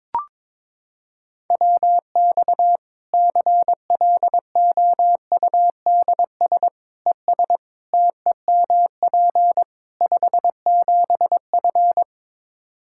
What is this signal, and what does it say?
Morse code, heard as a keyed tone, about 21 wpm, 700 Hz. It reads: WX CLOUDS ES TEMP 57F
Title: Practice QSO - Exchange 2 - Info - 22wpm